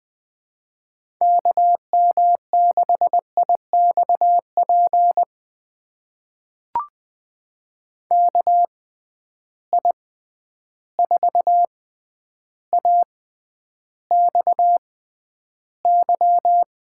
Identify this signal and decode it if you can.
Morse code, heard as a keyed tone, about 20 wpm, spaced as for 10 wpm, 700 Hz. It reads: KM6IXP KI4AXY
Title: USA Call Signs - 6 Characters - 6x Character Spacing - 20wpm